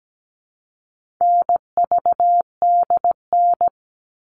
Morse code, heard as a keyed tone, about 17 wpm, 700 Hz. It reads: NVDN
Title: All Cut Numbers - 4-Digit 17wpm